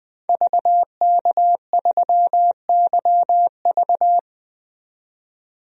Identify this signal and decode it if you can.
Morse code, heard as a keyed tone, about 20 wpm, 700 Hz. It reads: VK3YV
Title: DXCC-100 Call Signs - 5 Characters - 20wpm